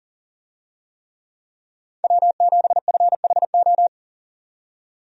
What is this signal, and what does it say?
Morse code, heard as a keyed tone, about 40 wpm, 700 Hz. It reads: W7FHO
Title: USA Call Signs - 5 Characters - 40wpm